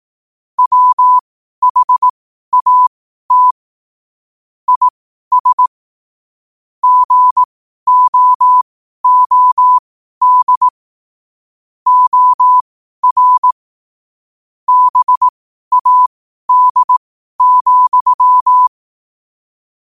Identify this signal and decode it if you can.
Morse code, heard as a keyed tone, about 18 wpm, spaced as for 12 wpm, 1000 Hz. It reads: WHAT IS GOOD OR BAD,
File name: qotdfiles / 12wpm